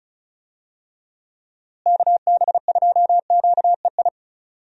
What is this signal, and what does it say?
Morse code, heard as a keyed tone, about 36 wpm, 700 Hz. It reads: KB2QEI
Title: USA Call Signs - 6 Characters - Repeated 2x - 35wpm